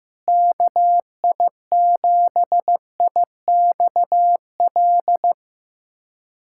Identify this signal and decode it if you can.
Morse code, heard as a keyed tone, about 15 wpm, 700 Hz. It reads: KI7IXL
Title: USA Callsign DE Callsign - 6 Characters - 15wpm